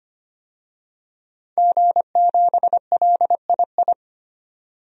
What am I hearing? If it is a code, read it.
Morse code, heard as a keyed tone, about 25 wpm, 700 Hz. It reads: G7LII
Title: DXCC-100 Call Signs - 5 Characters - 25wpm